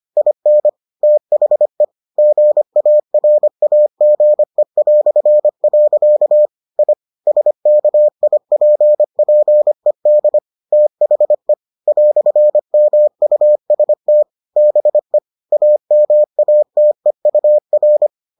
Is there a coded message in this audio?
Yes: Morse code from 0.2 to 18.1 s, heard as a keyed tone, about 25 wpm, 600 Hz. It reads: IN THE GARAGE". I SKIPPED THE "MUST BE AMATEUR